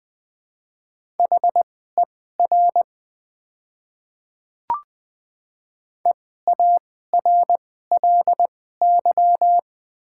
Morse code, heard as a keyed tone, about 20 wpm, spaced as for 14 wpm, 700 Hz. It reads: HER EARLY
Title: 2x Character Spacing - Top 500 Words - 20wpm